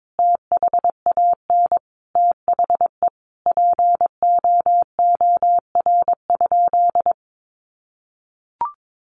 Morse code, heard as a keyed tone, about 22 wpm, 700 Hz. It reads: THAN THE POOR?
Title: Sentences from Top 1100 Words 22wpm